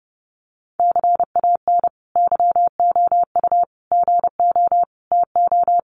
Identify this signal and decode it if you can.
Morse code, heard as a keyed tone, about 30 wpm, 700 Hz. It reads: CAN YOU GO TO
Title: Sentences from Top 1500 Words 30wpm